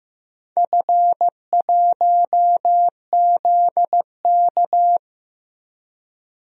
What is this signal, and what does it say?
Morse code, heard as a keyed tone, about 15 wpm, 700 Hz. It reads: F1ZK
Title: DXCC-100 Call Signs - 4 Characters - 15wpm